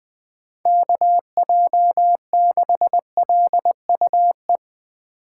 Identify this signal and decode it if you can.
Morse code, heard as a keyed tone, about 20 wpm, 700 Hz. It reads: KJ6LUE
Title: USA Call Signs - 4x Character Spacing - 20wpm